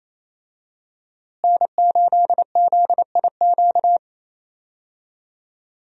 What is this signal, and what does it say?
Morse code, heard as a keyed tone, about 28 wpm, 700 Hz. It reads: N8ZIQ
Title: USA Call Signs - 5 Characters - Repeated 2x - 28wpm